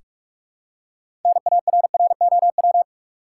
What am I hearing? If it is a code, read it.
Morse code, heard as a keyed tone, about 46 wpm, 700 Hz. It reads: NARROW
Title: Six Letter Words - 45wpm